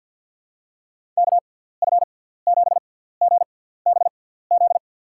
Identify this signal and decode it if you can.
Morse code, heard as a keyed tone, about 49 wpm, spaced as for 17 wpm, 700 Hz. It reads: KF7GBZ
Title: USA Call Signs - 6 Characters - 6x Character Spacing - 50wpm